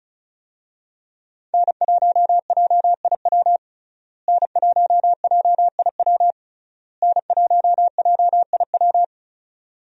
Morse code, heard as a keyed tone, about 34 wpm, 700 Hz. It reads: N1JIW N1JIW N1JIW
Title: USA Call Signs - 5 Characters - Repeated 3x - 35wpm